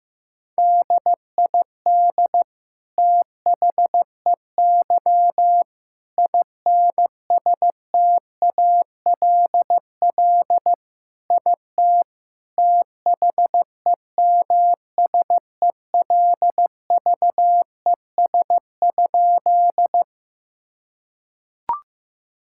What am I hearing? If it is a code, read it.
Morse code, heard as a keyed tone, about 14 wpm, 700 Hz. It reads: DID THEY INSTALL IT THEMSELVES?
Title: Sentences from Top 1600 Words 15wpm